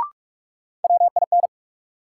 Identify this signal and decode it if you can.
Morse code, heard as a keyed tone, about 46 wpm, 700 Hz. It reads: WIN
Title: Sentences from Top 1500 Words - New Word Drill - 45wpm